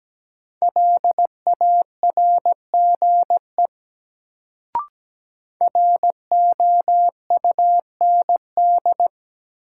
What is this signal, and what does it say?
Morse code, heard as a keyed tone, about 17 wpm, 700 Hz. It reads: LARGE ROUND
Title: Sentences from Top 200 Words - New Word Drill - 17wpm